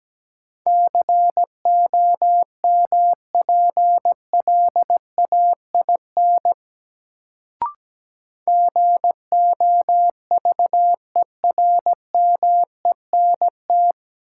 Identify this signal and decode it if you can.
Morse code, heard as a keyed tone, about 17 wpm, 700 Hz. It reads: COMPLAIN GOVERMENT
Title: Sentences from Top 1300 Words - New Word Drill Alt - 17wpm